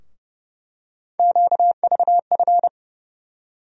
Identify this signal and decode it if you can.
Morse code, heard as a keyed tone, about 32 wpm, 700 Hz. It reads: QVF
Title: USA Call Sign Postfixes - 3 Letters 30wpm